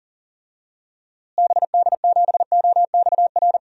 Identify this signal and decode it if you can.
Morse code, heard as a keyed tone, about 37 wpm, 700 Hz. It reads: BD7OXR
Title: DXCC-100 Call Signs - 6 Characters - Repeated 2x - 40wpm